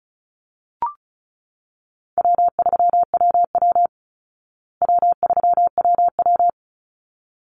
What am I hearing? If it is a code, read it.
Morse code, heard as a keyed tone, about 34 wpm, 700 Hz. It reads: W3WW W3WW
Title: Easy Call Signs Repeated 2x - 35wpm